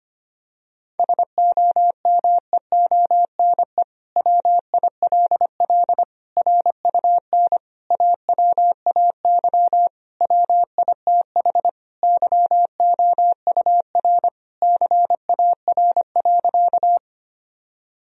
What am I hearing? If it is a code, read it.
Morse code, heard as a keyed tone, about 25 wpm, 700 Hz. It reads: SOMEONE WILL RUN AWAY WITH YOUR CAR.
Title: Sentences from Top 1000 Words - Review Part 1 - 25wpm